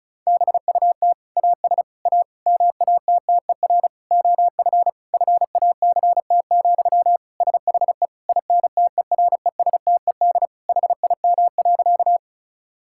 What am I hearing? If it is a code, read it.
Morse code, heard as a keyed tone, about 36 wpm, 700 Hz. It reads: BUT AS A MATTER OF FACT, SHE INTERESTED HIM.
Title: Sentences from Top 1000 Words - Review Part 6 - 35wpm